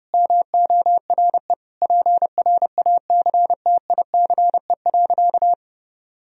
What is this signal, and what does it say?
Morse code, heard as a keyed tone, about 30 wpm, 700 Hz. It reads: MORE PRACTICE.
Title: Sentences from Top 1000 Words - Review Part 4 - 30wpm